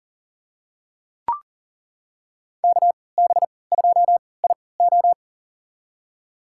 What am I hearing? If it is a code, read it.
Morse code, heard as a keyed tone, about 38 wpm, spaced as for 22 wpm, 700 Hz. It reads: KB2IO
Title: USA Call Signs - 5 Characters - 3x Character Spacing - 40wpm